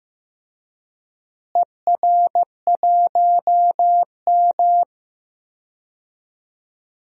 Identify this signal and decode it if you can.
Morse code, heard as a keyed tone, about 15 wpm, 700 Hz. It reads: ER1M
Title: DXCC-100 Call Signs - 4 Characters - Repeated 3x - 15wpm